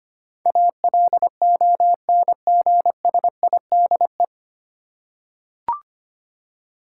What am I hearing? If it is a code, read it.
Morse code, heard as a keyed tone, about 25 wpm, 700 Hz. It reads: ALONGSIDE
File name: practice / 25wpm / 9 Letter Words - First 3 Letters Given - 25wpm